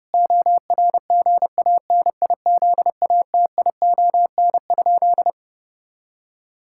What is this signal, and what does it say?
Morse code, heard as a keyed tone, about 30 wpm, 700 Hz. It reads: ORGANIZATION?
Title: Sentences from Top 1000 Words - Review Part 5 - 30wpm